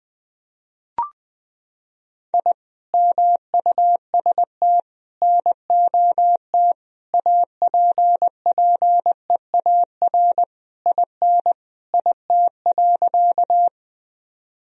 Morse code, heard as a keyed tone, about 21 wpm, 700 Hz. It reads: I MUST NOT APPEAR IN IT.
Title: Sentences from Top 1000 Words - Review Part 2 - 20wpm